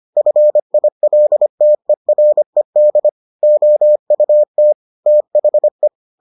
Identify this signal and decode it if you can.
Morse code, heard as a keyed tone, about 25 wpm, 600 Hz. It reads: FILTERED OUT THE